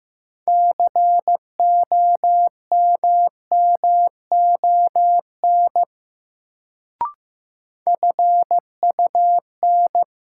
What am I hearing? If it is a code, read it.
Morse code, heard as a keyed tone, about 15 wpm, 700 Hz. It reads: COMMON FUN
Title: Sentences from Top 700 Words - New Word Drill - 15wpm